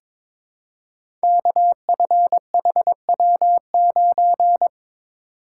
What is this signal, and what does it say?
Morse code, heard as a keyed tone, about 22 wpm, 700 Hz. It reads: KFHW9
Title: Code Groups - 5 Letter:Numbers - 22wpm